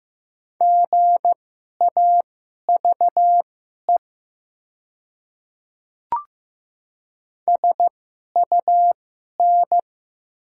Morse code, heard as a keyed tone, about 15 wpm, spaced as for 11 wpm, 700 Hz. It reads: GAVE SUN
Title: 2x Character Spacing - Top 500 Words - 15wpm